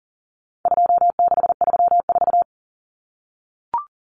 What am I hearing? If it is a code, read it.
Morse code, heard as a keyed tone, about 40 wpm, 700 Hz. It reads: 2634
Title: Numbers - 4-Digit 40wpm